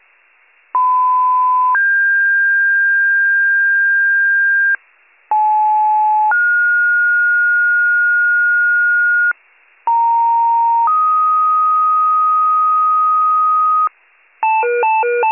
Two-tone page, heard as a one-second tone, then a three-second tone, about 1010 then 1600 Hz, about 860 then 1430 Hz, about 930 then 1250 Hz, with noise. It is heard from 0.7 to 13.9 s.